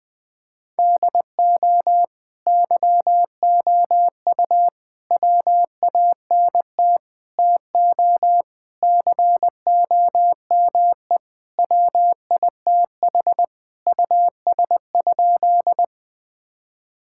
Morse code, heard as a keyed tone, about 20 wpm, 700 Hz. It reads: DO YOU WANT TO COME WITH US?
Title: Sentences with 3-Word Phrases - 20wpm